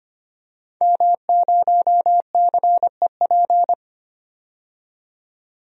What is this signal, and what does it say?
Morse code, heard as a keyed tone, about 25 wpm, 700 Hz. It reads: M0CEP
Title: DXCC-100 Call Signs - 5 Characters - 25wpm